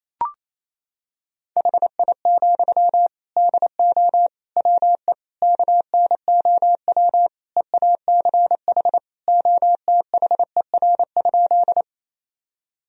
Morse code, heard as a keyed tone, about 28 wpm, 700 Hz. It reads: HI, DO WE KNOW EACH OTHER?
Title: Sentences from Top 1000 Words - Review Part 3 - 28wpm